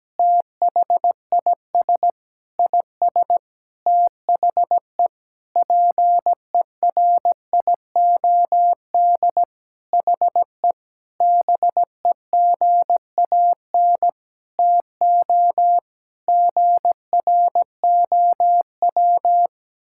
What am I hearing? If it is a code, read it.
Morse code, heard as a keyed tone, about 17 wpm, 700 Hz. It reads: THIS IS THE PERIOD HE BEGAN TO GROW
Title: Sentences from Top 1000 Words - Review Part 5 - 17wpm